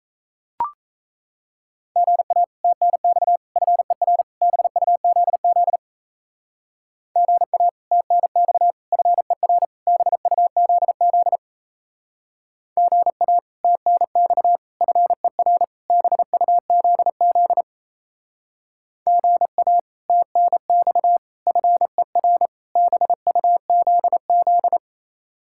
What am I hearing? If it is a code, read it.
Morse code, heard as a keyed tone, about 35 wpm, 700 Hz. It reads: GA TNX FER BUZZ GA TNX FER BUZZ GA TNX FER BUZZ GA TNX FER BUZZ
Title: Speed-Racing Common QSO Phrases - 28wpm